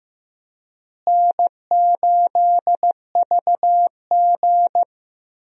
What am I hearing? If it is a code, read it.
Morse code, heard as a keyed tone, about 15 wpm, 700 Hz. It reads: N8VG
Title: USA Call Signs - 4 Characters - Repeated 2x - 15wpm